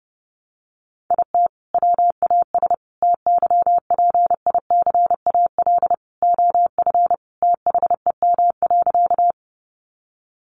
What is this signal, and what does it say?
Morse code, heard as a keyed tone, about 29 wpm, 700 Hz. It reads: IT WAS TYPICAL OF THEM.
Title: Sentences from Top 1100 Words 30wpm